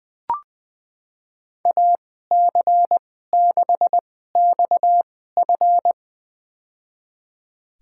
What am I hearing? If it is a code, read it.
Morse code, heard as a keyed tone, about 20 wpm, spaced as for 14 wpm, 700 Hz. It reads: AC6XF
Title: USA Call Signs - 5 Characters - 2x Character Spacing - 20wpm